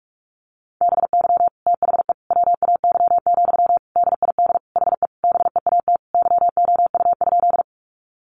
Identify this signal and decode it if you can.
Morse code, heard as a keyed tone, about 44 wpm, 700 Hz. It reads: BY THE WAY, DID HE BEAT YOU?
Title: Sentences from Top 1300 Words 45wpm